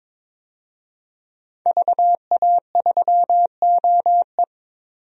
Morse code, heard as a keyed tone, about 23 wpm, 700 Hz. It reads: VA3OE
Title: DXCC-100 Call Signs - 5 Characters - 22wpm